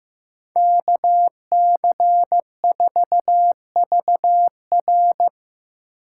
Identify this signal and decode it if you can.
Morse code, heard as a keyed tone, about 15 wpm, 700 Hz. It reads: K C 4 V R
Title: USA Call Signs - 5 Characters - 8x Character Spacing - 15wpm